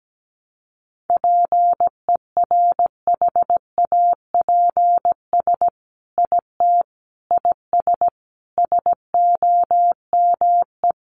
Morse code, heard as a keyed tone, about 17 wpm, 700 Hz. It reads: PERHAPS IT IS SOME
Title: Sentences from Top 800 Words 17wpm